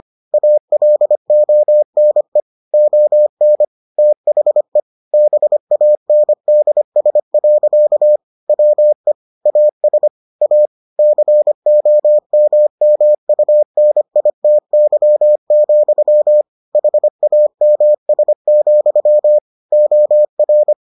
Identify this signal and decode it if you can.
Morse code, heard as a keyed tone, about 25 wpm, 600 Hz. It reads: ALONE ON THE BANDS. WE AS A COMMUNITY, HAMS, OR